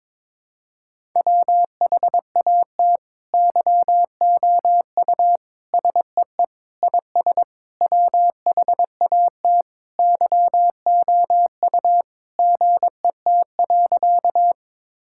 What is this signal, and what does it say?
Morse code, heard as a keyed tone, about 21 wpm, 700 Hz. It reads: WHAT YOU SEE IS WHAT YOU GET.
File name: Sentences with 7-Word Phrases - 22wpm